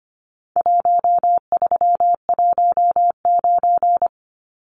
Morse code, heard as a keyed tone, about 25 wpm, 700 Hz. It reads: 1319
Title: Numbers - Contest Style - 4-Digit 25wpm